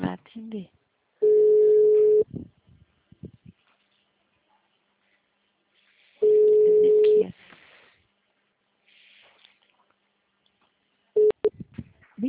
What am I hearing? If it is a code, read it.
Ringback tone (Europe, 425 Hz), with speech.